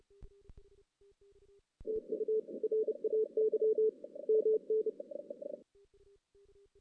Morse code, heard as a keyed tone, about 30 wpm, 410 Hz. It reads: TU RAY MN
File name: 2016 JAN CW NAQP